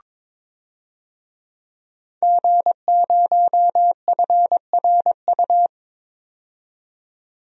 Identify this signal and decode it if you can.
Morse code, heard as a keyed tone, about 22 wpm, 700 Hz. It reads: G0FRU
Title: DXCC-100 Call Signs - 5 Characters - 22wpm